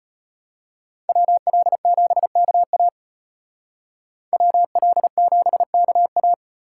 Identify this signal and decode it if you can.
Morse code, heard as a keyed tone, about 32 wpm, 700 Hz. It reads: WL7KA WL7KA